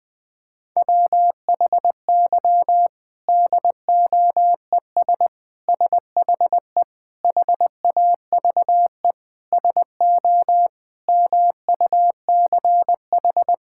Morse code, heard as a keyed tone, about 20 wpm, 700 Hz. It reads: WHY DOES SHE HAVE SO MUCH
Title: Sentences from Top 1000 Words - Review Part 6 - 20wpm